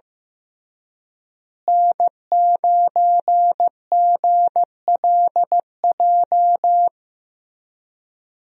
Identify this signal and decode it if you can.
Morse code, heard as a keyed tone, about 15 wpm, 700 Hz. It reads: N9GLJ
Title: USA Call Signs - 5 Characters - Repeated 3x - 15wpm